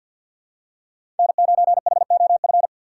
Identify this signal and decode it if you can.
Morse code, heard as a keyed tone, about 50 wpm, 700 Hz. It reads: N9HOF
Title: USA Call Signs - 5 Characters - 3x Character Spacing - 50wpm